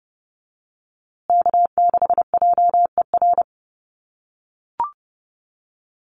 Morse code, heard as a keyed tone, about 30 wpm, 700 Hz. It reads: K6JER
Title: USA Call Signs Repeated 3x 30wpm